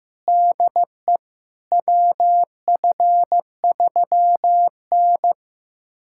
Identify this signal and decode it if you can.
Morse code, heard as a keyed tone, about 16 wpm, 700 Hz. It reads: DE WF3N
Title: USA Callsign DE Callsign - 4 Characters - 15wpm